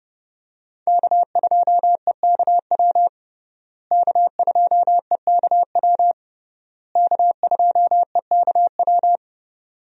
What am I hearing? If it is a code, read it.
Morse code, heard as a keyed tone, about 30 wpm, 700 Hz. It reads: K2EKW K2EKW K2EKW
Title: USA Call Signs - 5 Characters - Repeated 3x - 30wpm